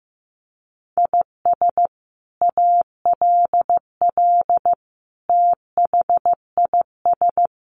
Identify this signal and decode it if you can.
Morse code, heard as a keyed tone, about 15 wpm, 700 Hz. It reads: IS ALL THIS
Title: Sentences from Top 1400 Words 15wpm